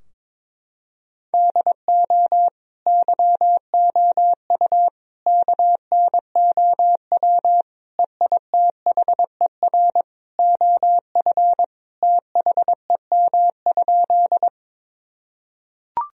Morse code, heard as a keyed tone, about 22 wpm, 700 Hz. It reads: DO YOU KNOW EITHER OF THEM?
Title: Sentences from Top 1000 Words - Review Part 4 - 22wpm